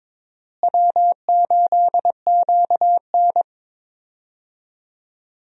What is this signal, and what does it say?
Morse code, heard as a keyed tone, about 22 wpm, 700 Hz. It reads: W8QN
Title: USA Call Signs - 4 Characters - Repeated 3x - 22wpm